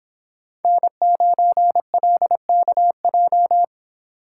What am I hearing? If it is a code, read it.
Morse code, heard as a keyed tone, about 25 wpm, 700 Hz. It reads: N9LKJ